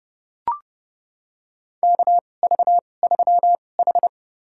Morse code, heard as a keyed tone, about 30 wpm, spaced as for 22 wpm, 700 Hz. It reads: KV3H
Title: USA Call Signs - 4 Characters - 2x Character Spacing - 30wpm